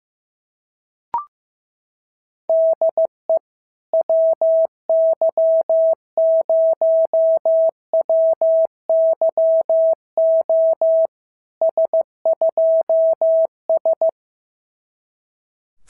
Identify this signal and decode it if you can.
Morse code, heard as a keyed tone, about 15 wpm, 650 Hz. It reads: DE WY0WYO S2S
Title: SOTA QSO Practice 15wpm